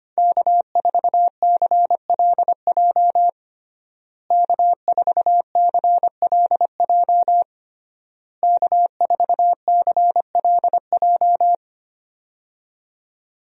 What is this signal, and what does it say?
Morse code, heard as a keyed tone, about 25 wpm, 700 Hz. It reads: K4CLJ K4CLJ K4CLJ
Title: USA Call Signs - 5 Characters - Repeated 3x - 25wpm